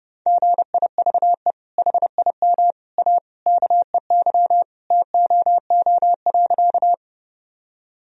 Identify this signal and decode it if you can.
Morse code, heard as a keyed tone, about 30 wpm, 700 Hz. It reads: GIVE HIM A KEY TOO.
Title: Sentences from Top 1000 Words - Review Part 4 - 30wpm